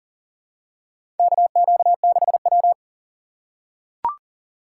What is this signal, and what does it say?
Morse code, heard as a keyed tone, about 40 wpm, 700 Hz. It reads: KQ6W
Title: USA Call Signs - 4 Characters - 3x Character Spacing - 40wpm